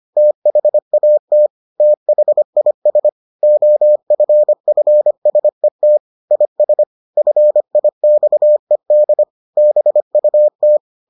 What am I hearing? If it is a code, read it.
Morse code, heard as a keyed tone, about 25 wpm, 600 Hz. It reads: THAT THIS OFFSET IS FIXED BUT